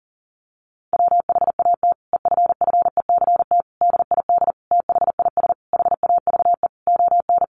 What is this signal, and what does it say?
Morse code, heard as a keyed tone, about 40 wpm, 700 Hz. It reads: WHAT EFFECT DID THIS HAVE ON
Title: Sentences from Top 1000 Words 40wpm